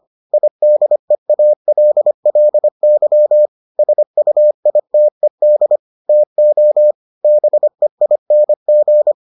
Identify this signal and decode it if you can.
Morse code, heard as a keyed tone, about 25 wpm, 600 Hz. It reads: IDEALLY SUITED TO BEING